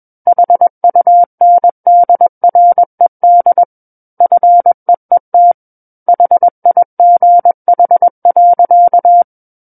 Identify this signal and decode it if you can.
Morse code, heard as a keyed tone, about 21 wpm, 700 Hz. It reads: HUNDRED FEET HIGH.
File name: Book One / 12-Chapter 12 - 21 WPM